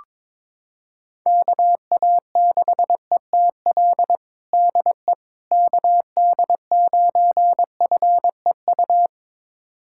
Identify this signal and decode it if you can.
Morse code, heard as a keyed tone, about 22 wpm, 700 Hz. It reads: KA6ETL DE KD9FEU
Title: USA Callsign DE Callsign - 6 Characters - 22wpm